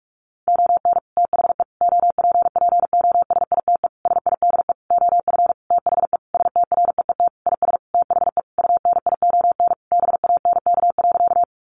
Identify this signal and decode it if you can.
Morse code, heard as a keyed tone, about 44 wpm, 700 Hz. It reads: ON THE OPPOSITE SIDE OF THE STREET IS THE UNION BANK.
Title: Sentences from Top 1400 Words 45wpm